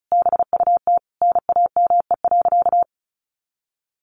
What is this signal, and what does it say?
Morse code, heard as a keyed tone, about 34 wpm, 700 Hz. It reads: BUT NAME.
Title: Sentences from Top 1500 Words 35wpm